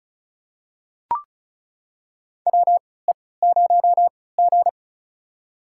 Morse code, heard as a keyed tone, about 36 wpm, spaced as for 20 wpm, 700 Hz. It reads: WE0G
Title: USA Call Signs - 4 Characters - 3x Character Spacing - 35wpm